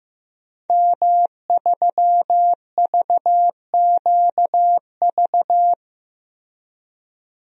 Morse code, heard as a keyed tone, about 15 wpm, 700 Hz. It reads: M3VQV
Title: DXCC-100 Call Signs - 5 Characters - Repeated 3x - 15wpm